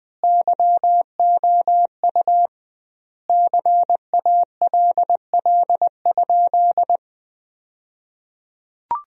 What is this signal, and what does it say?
Morse code, heard as a keyed tone, about 20 wpm, 700 Hz. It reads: YOU CALL?
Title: 2x Word-Spacing - Sentences from Top 100 Words - 20wpm